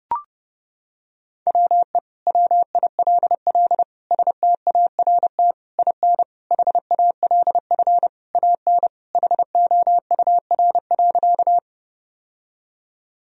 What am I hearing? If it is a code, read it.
Morse code, heard as a keyed tone, about 29 wpm, 700 Hz. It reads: WE WILL START IN HALF AN HOUR.
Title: Sentences from Top 1000 Words - Review Part 1 - 30wpm